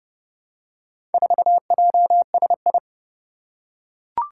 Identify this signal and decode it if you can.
Morse code, heard as a keyed tone, about 30 wpm, 700 Hz. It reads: 4JSI
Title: Four Challenging Letter-Number Combinations 30wpm